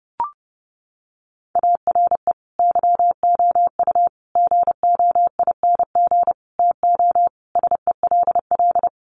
Morse code, heard as a keyed tone, about 30 wpm, 700 Hz. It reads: ARE YOU GOING TO SELL
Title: Sentences from Top 1200 Words 30wpm